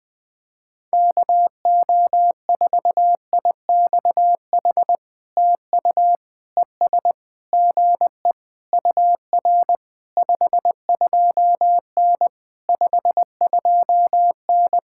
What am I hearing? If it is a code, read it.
Morse code, heard as a keyed tone, about 19 wpm, 700 Hz. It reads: KO4IXH TU ES GE UR 52N 52N
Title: POTA QSO Practice - 20wpm